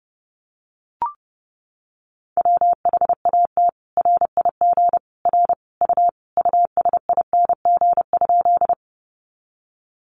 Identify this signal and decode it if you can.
Morse code, heard as a keyed tone, about 30 wpm, 700 Hz. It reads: WHAT RIG R U USING?
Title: Common QSO Phrases 30wpm